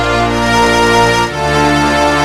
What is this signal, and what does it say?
There is no code.